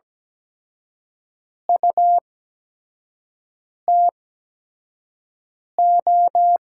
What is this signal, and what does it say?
Morse code, heard as a keyed tone, about 17 wpm, spaced as for 5 wpm, 700 Hz. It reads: UTO
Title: USA Call Signs - 8x Character Spacing - 17wpm